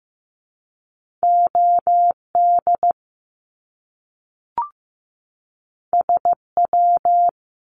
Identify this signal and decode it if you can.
Morse code, heard as a keyed tone, about 15 wpm, 700 Hz. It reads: OD SW
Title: Top Two Letter Combinations in Words 15wpm